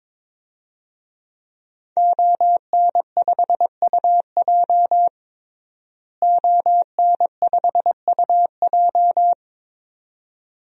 Morse code, heard as a keyed tone, about 22 wpm, 700 Hz. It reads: ON5UJ ON5UJ
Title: DXCC-100 Call Signs - 5 Characters - Repeated 2x - 22wpm